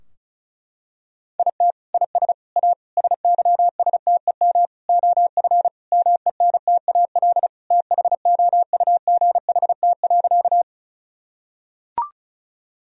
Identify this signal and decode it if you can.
Morse code, heard as a keyed tone, about 34 wpm, 700 Hz. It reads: IT IS A SYSTEM OF MENTAL THOUGHT.
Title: Sentences from Top 1000 Words - Review Part 4 - 35wpm